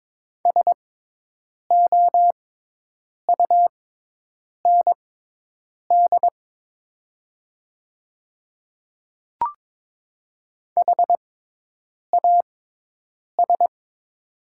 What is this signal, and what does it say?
Morse code, heard as a keyed tone, about 22 wpm, spaced as for 8 wpm, 700 Hz. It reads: SOUND HAS
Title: 6x Character Spacing - Top 300 Words - 22wpm